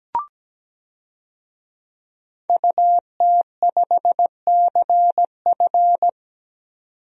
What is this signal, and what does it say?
Morse code, heard as a keyed tone, about 17 wpm, 700 Hz. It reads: UT5CF
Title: DXCC-100 Call Signs - 5 Characters - 17wpm